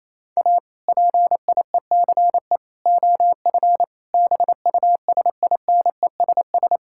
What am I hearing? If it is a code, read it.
Morse code, heard as a keyed tone, about 28 wpm, 700 Hz. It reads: A PIECE OF BUSINESS
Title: Sentences from Top 1000 Words - Review Part 5 - 28wpm